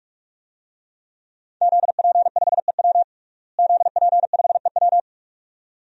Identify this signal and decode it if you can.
Morse code, heard as a keyed tone, about 46 wpm, 700 Hz. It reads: ZP5EW ZP5EW
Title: DXCC-100 Call Signs - 5 Characters - Repeated 2x - 45wpm